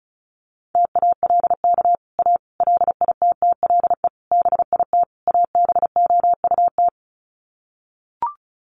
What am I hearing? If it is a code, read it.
Morse code, heard as a keyed tone, about 34 wpm, 700 Hz. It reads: TALK A LITTLE BIT ABOUT
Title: Sets of 5 Words Encore 35wpm